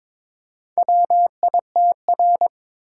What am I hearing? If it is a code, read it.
Morse code, heard as a keyed tone, about 22 wpm, 700 Hz. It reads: WITR
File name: practice / 22wpm / Four Letter-Number Combinations - 22wpm